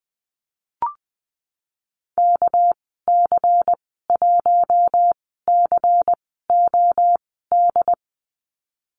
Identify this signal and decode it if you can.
Morse code, heard as a keyed tone, about 20 wpm, 700 Hz. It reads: K C 1 C O D
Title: 231 Lesson 37g - Letter x - Call Signs 20wpm